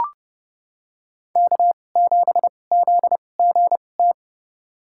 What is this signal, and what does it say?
Morse code, heard as a keyed tone, about 30 wpm, spaced as for 22 wpm, 700 Hz. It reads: K7ZGT
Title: USA Call Signs - 2x Character Spacing - 30wpm